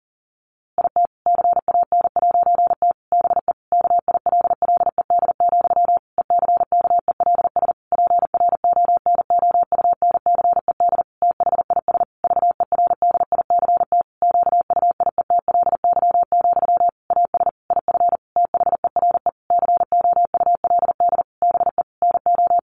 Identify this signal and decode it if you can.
Morse code, heard as a keyed tone, about 40 wpm, 700 Hz. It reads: IT CAN'T BE KILLED, ECKELS PRONOUNCED THIS VERDICT QUIETLY, AS IF THERE COULD BE NO
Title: A Sound Of Thunder 40wpm